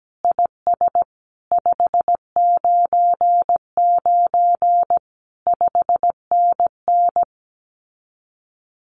Morse code, heard as a keyed tone, about 17 wpm, 700 Hz. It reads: IS 599 5NN
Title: Common QSO Phrases Encore 17wpm